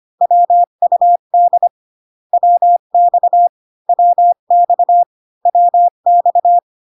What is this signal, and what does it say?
Morse code, heard as a keyed tone, about 25 wpm, 700 Hz. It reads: WUD WX WX WX